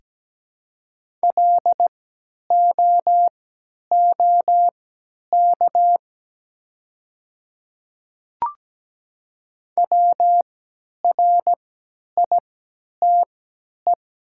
Morse code, heard as a keyed tone, about 17 wpm, spaced as for 10 wpm, 690 Hz. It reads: LOOK WRITE
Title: 3x Character Spacing - Top 100 Words - 17wpm